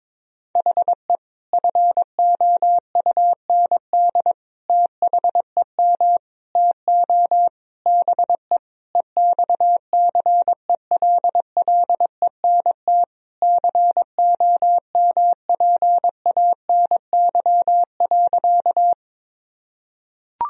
Morse code, heard as a keyed tone, about 22 wpm, 700 Hz. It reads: HE FOUND THEM TO BE EXCELLENT COMPANY.
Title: Sentences from Top 1000 Words - Review Part 3 - 22wpm